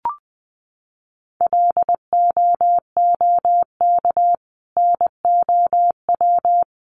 Morse code, heard as a keyed tone, about 19 wpm, 700 Hz. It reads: LOOK NOW
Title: Sentences from Top 1400 Words 20wpm